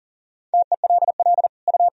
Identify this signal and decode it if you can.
Morse code, heard as a keyed tone, about 40 wpm, 700 Hz. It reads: TELL U
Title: Sentences from Top 1000 Words - Review Part 2 - 40wpm